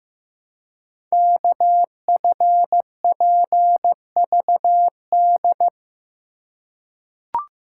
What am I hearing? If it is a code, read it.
Morse code, heard as a keyed tone, about 15 wpm, 700 Hz. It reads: KFPVD
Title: Code Groups - 5 Letters - 15wpm